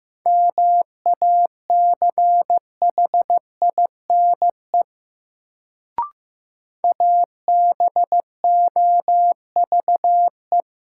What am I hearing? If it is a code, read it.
Morse code, heard as a keyed tone, about 15 wpm, 700 Hz. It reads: MACHINE ABOVE
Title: Sentences from Top 1000 Words - Review Part 2 - New Word Drill - 15wpm